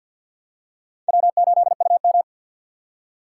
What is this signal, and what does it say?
Morse code, heard as a keyed tone, about 53 wpm, 700 Hz. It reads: W8UM
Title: USA Call Signs - 4x Character Spacing - 50wpm